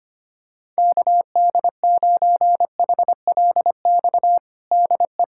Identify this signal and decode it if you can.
Morse code, heard as a keyed tone, about 25 wpm, 700 Hz. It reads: KD9HLX DE
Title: USA Callsign DE Callsign - 6 Characters - 25wpm